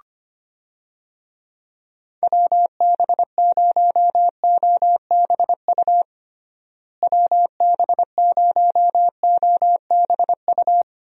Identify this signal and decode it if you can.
Morse code, heard as a keyed tone, about 25 wpm, 700 Hz. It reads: WB0OBU WB0OBU
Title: USA Call Signs - 6 Characters - Repeated 3x - 25wpm